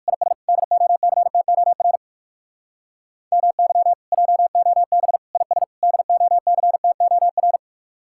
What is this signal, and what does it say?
Morse code, heard as a keyed tone, about 47 wpm, 700 Hz. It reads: IS DOCTOR MY JOB IS DOCTOR
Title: Speed-Racing Common QSO Phrases Encore - 45wpm